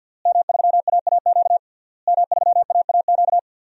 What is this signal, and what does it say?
Morse code, heard as a keyed tone, about 52 wpm, 700 Hz. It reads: M3AAQ M3AAQ
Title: DXCC-100 Call Signs - 5 Characters - Repeated 3x - 50wpm